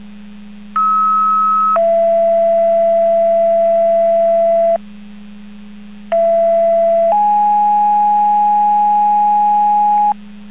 Two-tone page, heard as a one-second tone, then a three-second tone, about 1290 then 670 Hz, about 670 then 830 Hz. It is heard from 0.8 to 10.1 s.